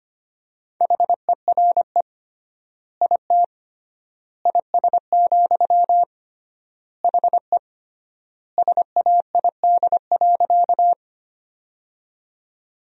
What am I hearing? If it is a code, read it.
Morse code, heard as a keyed tone, about 25 wpm, 700 Hz. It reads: HERE IT IS, HE SAID.
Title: 3x Word-Spacing - Sentences from Top 200 Words - 25wpm